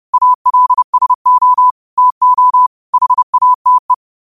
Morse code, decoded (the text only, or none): ALSO TO HATE